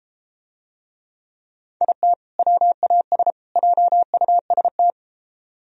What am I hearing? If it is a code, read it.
Morse code, heard as a keyed tone, about 33 wpm, 700 Hz. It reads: IT WAS JUST